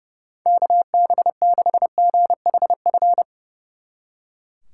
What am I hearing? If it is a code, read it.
Morse code, heard as a keyed tone, about 37 wpm, 700 Hz. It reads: KB6GHF
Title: Speed Racing - USA 2x3 Call Signs 30wpm